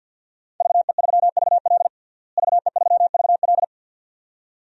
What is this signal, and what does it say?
Morse code, heard as a keyed tone, about 46 wpm, 700 Hz. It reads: VE3VL VE3VL
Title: DXCC-100 Call Signs - 5 Characters - Repeated 2x - 50wpm